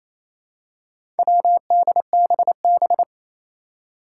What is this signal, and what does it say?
Morse code, heard as a keyed tone, about 28 wpm, 700 Hz. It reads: WDBB
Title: Four Challenging Letter-Number Combinations 28wpm